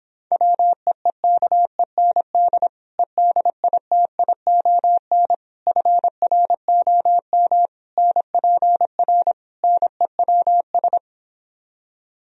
Morse code, heard as a keyed tone, about 26 wpm, 700 Hz. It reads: WEEKEND EDITION FROM NPR NEWS